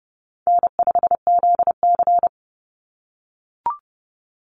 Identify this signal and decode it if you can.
Morse code, heard as a keyed tone, about 30 wpm, 700 Hz. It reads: N5ZC
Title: NA SOTA Chaser Callsigns 30wpm